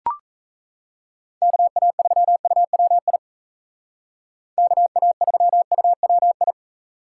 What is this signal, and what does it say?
Morse code, heard as a keyed tone, about 34 wpm, 700 Hz. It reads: KA3UWI KA3UWI
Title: Speed Racing - USA 2x3 Call Signs 28wpm